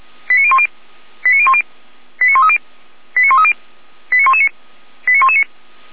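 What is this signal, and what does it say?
Five-tone selective call (ZVEI), heard as quick runs of five tones, tones of about 70 ms. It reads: 89010 89010 89130 89130 89119 89119
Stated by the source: ZVEI five-tone call received by SDR, from feuerwehr-orlamuende.org fire brigade audio server.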